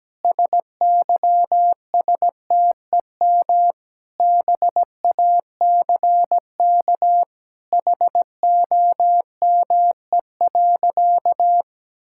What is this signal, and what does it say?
Morse code, heard as a keyed tone, about 17 wpm, 700 Hz. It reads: SYSTEM BACK HOME.
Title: Sentences from Top 1000 Words - Review Part 5 - 17wpm